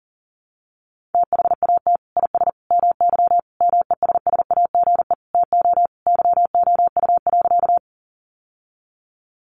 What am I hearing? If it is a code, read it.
Morse code, heard as a keyed tone, about 40 wpm, 700 Hz. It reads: THAT IS MY MESSAGE TO YOU.
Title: Sentences from Top 1100 Words 40wpm